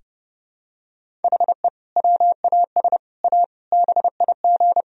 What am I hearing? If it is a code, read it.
Morse code, heard as a keyed tone, about 30 wpm, 700 Hz. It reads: HE WAS A BIG
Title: Sentences from Top 1000 Words - Review Part 1 - 30wpm